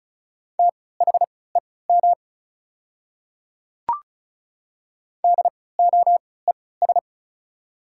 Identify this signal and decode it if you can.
Morse code, heard as a keyed tone, about 36 wpm, spaced as for 20 wpm, 690 Hz. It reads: THEM DOES
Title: 3x Character Spacing - Top 200 Words - 35wpm